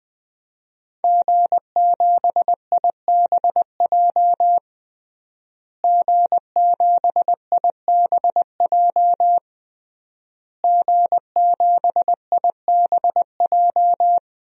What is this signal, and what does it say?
Morse code, heard as a keyed tone, about 19 wpm, 700 Hz. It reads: G7IBJ G7IBJ G7IBJ
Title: DXCC-100 Call Signs - 5 Characters - Repeated 3x - 20wpm